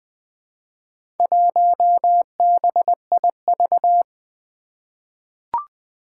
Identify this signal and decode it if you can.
Morse code, heard as a keyed tone, about 20 wpm, 700 Hz. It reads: 1BIV
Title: Four Challenging Letter-Number Combinations 20wpm